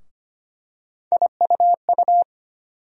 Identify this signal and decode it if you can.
Morse code, heard as a keyed tone, about 25 wpm, 700 Hz. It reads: IUU
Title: USA Call Sign Postfixes - 3 Letters 25wpm